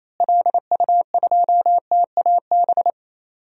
Morse code, heard as a keyed tone, about 28 wpm, 700 Hz. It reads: LU2TAB
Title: DXCC-100 Call Signs - 6 Characters - Repeated 2x - 28wpm